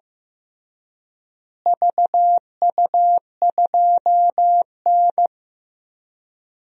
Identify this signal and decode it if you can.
Morse code, heard as a keyed tone, about 15 wpm, 700 Hz. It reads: VU2N
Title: DXCC-100 Call Signs - 4 Characters - Repeated 3x - 15wpm